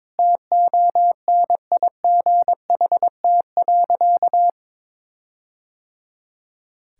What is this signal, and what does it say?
Morse code, heard as a keyed tone, about 22 wpm, 700 Hz. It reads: TONIGHT.
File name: practice / 22wpm / Sentences with 4-Word Phrases - 22wpm